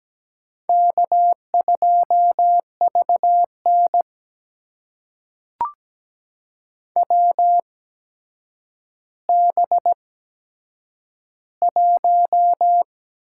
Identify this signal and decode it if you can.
Morse code, heard as a keyed tone, about 17 wpm, spaced as for 8 wpm, 700 Hz. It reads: K2VN WB1
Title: USA Call Signs - 8x Character Spacing - 17wpm